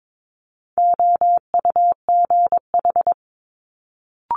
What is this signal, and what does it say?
Morse code, heard as a keyed tone, about 22 wpm, 700 Hz. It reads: OUGH
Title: Multi-letter Phonemes Quick Fire 22wpm